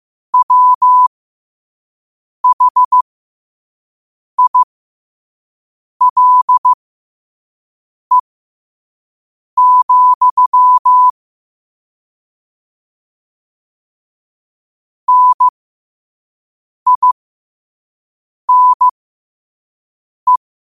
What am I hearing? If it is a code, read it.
Morse code, heard as a keyed tone, about 15 wpm, spaced as for 5 wpm, 1000 Hz. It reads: WHILE, NINE
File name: qotdfiles / 5wpm